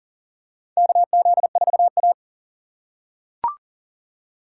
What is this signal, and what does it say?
Morse code, heard as a keyed tone, about 40 wpm, 700 Hz. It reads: KZ4A
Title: K1USN SST QSO Practice - 40wpm